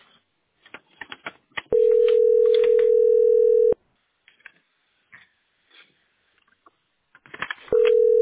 Ringback tone (North America), 440 + 480 Hz.